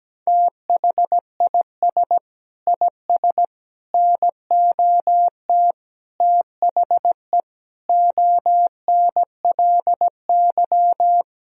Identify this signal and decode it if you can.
Morse code, heard as a keyed tone, about 17 wpm, 700 Hz. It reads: THIS IS NOT THE ONLY
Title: Sentences from Top 1000 Words - Review Part 2 - 17wpm